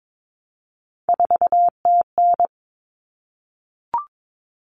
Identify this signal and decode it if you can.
Morse code, heard as a keyed tone, about 22 wpm, 700 Hz. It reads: 4TN
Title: Common Cut Numbers - 3-Digit 22wpm